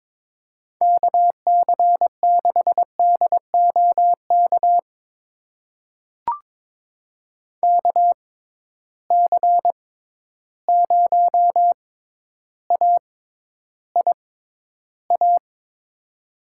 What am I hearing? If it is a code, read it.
Morse code, heard as a keyed tone, about 22 wpm, spaced as for 11 wpm, 700 Hz. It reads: KC6DOK KC0AIA
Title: USA Call Signs - 6 Characters - 6x Character Spacing - 22wpm